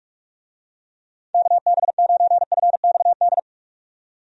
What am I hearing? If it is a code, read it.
Morse code, heard as a keyed tone, about 44 wpm, 700 Hz. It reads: KB9FXD
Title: USA Call Signs - 6 Characters - 45wpm